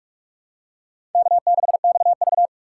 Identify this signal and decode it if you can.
Morse code, heard as a keyed tone, about 47 wpm, 700 Hz. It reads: K6XV
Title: USA Call Signs - 6x Character Spacing - 45wpm